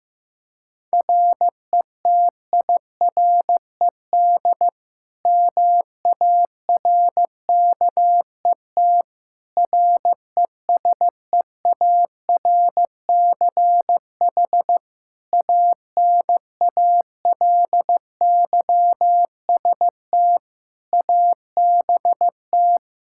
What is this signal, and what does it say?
Morse code, heard as a keyed tone, about 15 wpm, 700 Hz. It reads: RETIRED MARKET RESEARCH ANALYST ABT
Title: Practice QSO - Exchange 3 - More Info - 15wpm